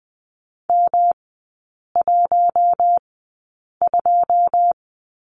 Morse code, heard as a keyed tone, about 20 wpm, spaced as for 8 wpm, 700 Hz. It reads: M12
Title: 121 Lesson 20e - Number 3 - Three Characters 20wpm